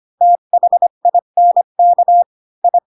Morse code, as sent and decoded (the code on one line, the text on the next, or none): - .... .. -. -.- / ..
THINK I